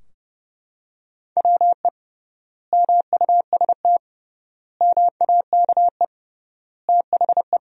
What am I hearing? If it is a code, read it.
Morse code, heard as a keyed tone, about 31 wpm, 700 Hz. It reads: WE MUST MAKE THE
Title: 3x Word-Spacing - Sentences from Top 400 Words - 30wpm